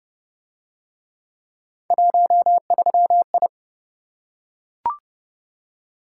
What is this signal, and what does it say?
Morse code, heard as a keyed tone, about 25 wpm, 700 Hz. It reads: 13I
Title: Speed-Racing - Three Letter-Number Combinations - 20wpm